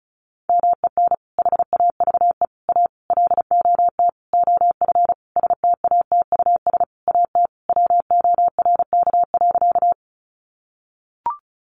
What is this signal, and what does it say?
Morse code, heard as a keyed tone, about 34 wpm, 700 Hz. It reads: MEN HAVE A LOT OF STATUS AT WORK.
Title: Sentences from Top 1400 Words 35wpm